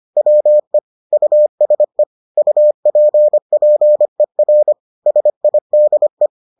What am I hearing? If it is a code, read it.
Morse code, heard as a keyed tone, about 25 wpm, 600 Hz. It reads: WE USE UPPER SIDE